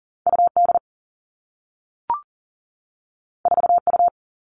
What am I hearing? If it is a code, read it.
Morse code, heard as a keyed tone, about 40 wpm, 700 Hz. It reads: UD 4U
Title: All Cut Numbers - 2-Digit 40wpm